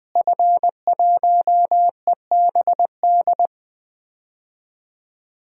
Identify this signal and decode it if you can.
Morse code, heard as a keyed tone, about 20 wpm, 700 Hz. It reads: F1EBD
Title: DXCC-100 Call Signs - 5 Characters - Repeated 3x - 20wpm